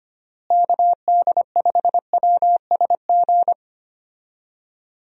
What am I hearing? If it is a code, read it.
Morse code, heard as a keyed tone, about 25 wpm, 700 Hz. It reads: KD5WSG
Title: USA Call Signs - 6 Characters - Repeated 3x - 25wpm